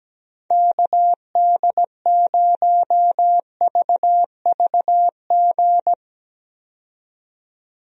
Morse code, heard as a keyed tone, about 17 wpm, 700 Hz. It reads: KD0VVG
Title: USA Call Signs - 6 Characters - 17wpm